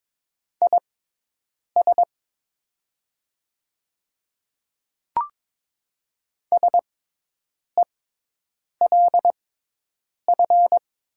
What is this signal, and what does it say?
Morse code, heard as a keyed tone, about 22 wpm, spaced as for 8 wpm, 700 Hz. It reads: IS SELF
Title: 6x Character Spacing - Top 300 Words - 22wpm